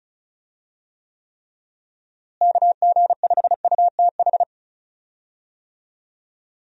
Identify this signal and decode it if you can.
Morse code, heard as a keyed tone, about 36 wpm, 700 Hz. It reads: KG5UTH
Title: USA Call Signs - 6 Characters - Repeated 3x - 35wpm